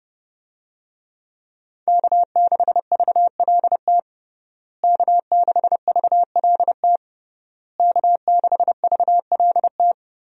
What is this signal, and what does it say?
Morse code, heard as a keyed tone, about 32 wpm, 700 Hz. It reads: K6VLT K6VLT K6VLT
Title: USA Call Signs - 5 Characters - Repeated 3x - 30wpm